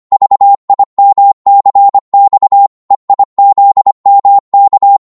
Morse code, heard as a keyed tone, about 25 wpm, 800 Hz. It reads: VIMCX EIZMK
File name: Morse / ltr-25wpm